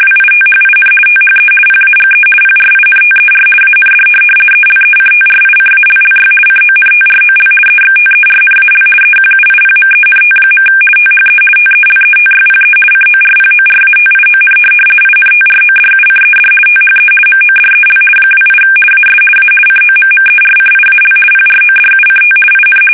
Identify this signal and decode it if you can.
RTTY, about 50 baud. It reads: WITHIN THE PAGE PRINTER AND ELECTRICALLY CONTROL EXTERNAL EQUIPMENT. THE TELETYPE CORPORATION INTRODUCED THE MODEL 28 AS A COMMERCIAL PRODUCT IN
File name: fsk850-50B / teletype-corp